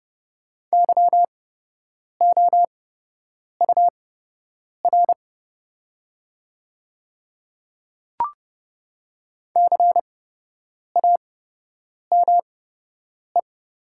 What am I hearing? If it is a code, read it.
Morse code, heard as a keyed tone, about 30 wpm, spaced as for 8 wpm, 700 Hz. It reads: YOUR CAME
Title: 8x Character Spacing - Top 300 Words - 30wpm